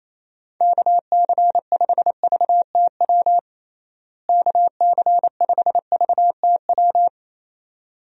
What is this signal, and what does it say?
Morse code, heard as a keyed tone, about 28 wpm, 700 Hz. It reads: KC5VTW KC5VTW
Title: USA Call Signs - 6 Characters - Repeated 2x - 28wpm